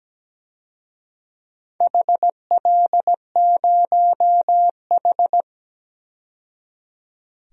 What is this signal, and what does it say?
Morse code, heard as a keyed tone, about 17 wpm, 700 Hz. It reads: HL0H
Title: DXCC-100 Call Signs - 4 Characters - 17wpm